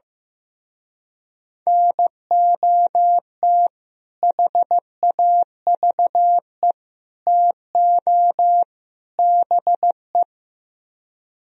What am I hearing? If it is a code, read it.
Morse code, heard as a keyed tone, about 16 wpm, 700 Hz. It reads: NOT HAVE TO BE
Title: Speed-Racing - Sets of 4 Words 15wpm